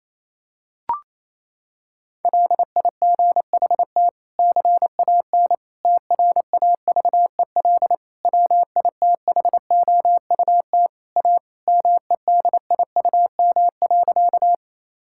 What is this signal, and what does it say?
Morse code, heard as a keyed tone, about 28 wpm, 700 Hz. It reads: LIGHT CAN TRAVEL WITHOUT A MEDIUM.
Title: Sentences from Top 1000 Words - Review Part 6 - 28wpm